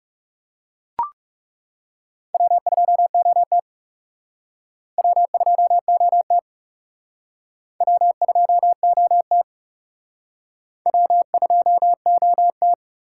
Morse code, heard as a keyed tone, about 40 wpm, 700 Hz. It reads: W2OT W2OT W2OT W2OT